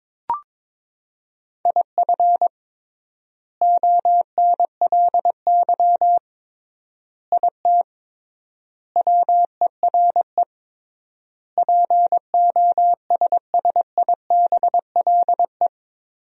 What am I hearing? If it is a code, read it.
Morse code, heard as a keyed tone, about 22 wpm, 700 Hz. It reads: IF ONLY IT WERE POSSIBLE
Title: 3x Word-Spacing - Sentences from Top 500 Words - 22wpm